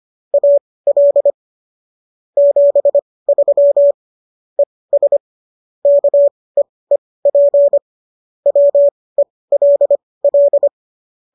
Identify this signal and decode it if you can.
Morse code, heard as a keyed tone, about 25 wpm, spaced as for 18 wpm, 570 Hz. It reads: AL 73 ES KEEP WELL